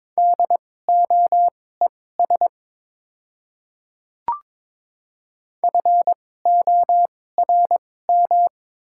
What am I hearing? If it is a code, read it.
Morse code, heard as a keyed tone, about 22 wpm, spaced as for 16 wpm, 700 Hz. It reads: DOES FORM
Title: 2x Character Spacing - Top 200 Words - 22wpm